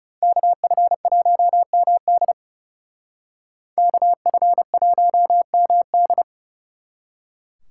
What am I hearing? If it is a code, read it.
Morse code, heard as a keyed tone, about 38 wpm, 700 Hz. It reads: KF1MD KF1MD